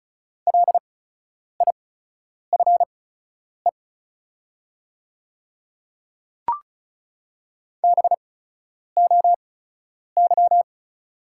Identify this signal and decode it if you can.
Morse code, heard as a keyed tone, about 34 wpm, spaced as for 10 wpm, 690 Hz. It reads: LIFE BOY
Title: 8x Character Spacing - Top 500 Words - 35wpm